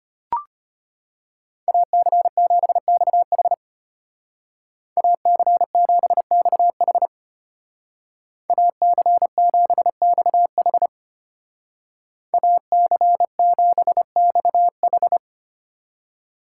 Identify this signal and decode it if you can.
Morse code, heard as a keyed tone, about 33 wpm, 700 Hz. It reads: AC7XH AC7XH AC7XH AC7XH